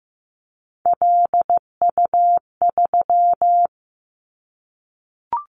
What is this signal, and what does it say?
Morse code, heard as a keyed tone, about 15 wpm, 700 Hz. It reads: LU3
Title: Callsign Prefixes for Top 100 DXCC - 15wpm